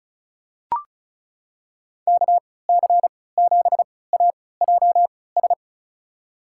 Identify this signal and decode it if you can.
Morse code, heard as a keyed tone, about 36 wpm, spaced as for 20 wpm, 700 Hz. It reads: KC7AJS
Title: USA Call Signs - 6 Characters - 3x Character Spacing - 35wpm